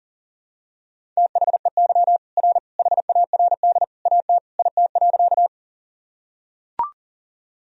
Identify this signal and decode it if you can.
Morse code, heard as a keyed tone, about 40 wpm, 700 Hz. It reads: THEY R HARD AT IT.
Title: Sentences from Top 300 Words and Abbreviations 40wpm